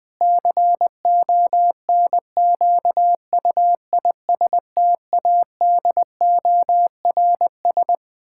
Morse code, heard as a keyed tone, about 20 wpm, 700 Hz. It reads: CONQUISTADORS
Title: Pangram Challenge v2 - 20wpm